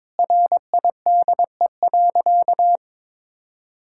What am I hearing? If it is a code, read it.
Morse code, heard as a keyed tone, about 22 wpm, 700 Hz. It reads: RIDE.
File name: Difficult Sentences with Hints - 22wpm